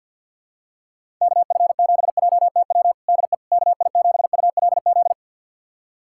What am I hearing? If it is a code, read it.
Morse code, heard as a keyed tone, about 48 wpm, 700 Hz. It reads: KF7JTW DE KI7ULZ
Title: USA Callsign DE Callsign 50wpm